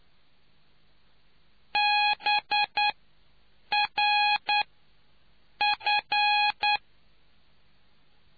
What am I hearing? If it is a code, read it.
Morse code, heard as a keyed tone, about 9 wpm, spaced as for 6 wpm, 830 Hz. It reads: BRF